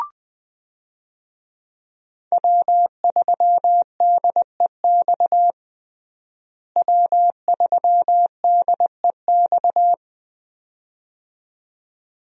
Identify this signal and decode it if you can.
Morse code, heard as a keyed tone, about 20 wpm, 700 Hz. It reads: W3DEX W3DEX
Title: USA Call Signs - 5 Characters - Repeated 2x - 20wpm